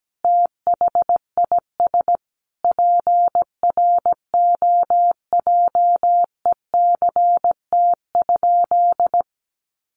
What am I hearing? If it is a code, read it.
Morse code, heard as a keyed tone, about 17 wpm, 700 Hz. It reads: THIS PROJECT?
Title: Sentences from Top 1400 Words 17wpm